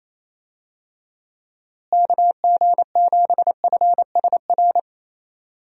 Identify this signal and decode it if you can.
Morse code, heard as a keyed tone, about 28 wpm, 700 Hz. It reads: KG7FSR
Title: USA Call Signs - 6 Characters - Repeated 2x - 28wpm